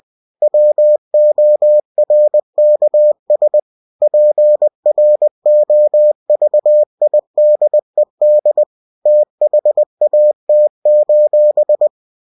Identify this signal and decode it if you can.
Morse code, heard as a keyed tone, about 20 wpm, 600 Hz. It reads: WORKS PROVIDED THAT:
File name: cw-books / winnie-the-pooh / winnie-20wpm-page-0116